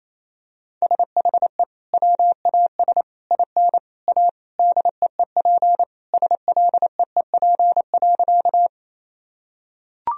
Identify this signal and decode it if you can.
Morse code, heard as a keyed tone, about 28 wpm, 700 Hz. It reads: SHE WAS IN A DEEP SLEEP.
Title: Sentences from Top 1000 Words - Review Part 3 - 28wpm